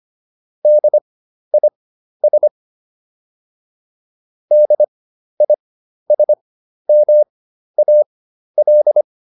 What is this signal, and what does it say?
Morse code, heard as a keyed tone, about 25 wpm, spaced as for 12 wpm, 600 Hz. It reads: DIS DISMAL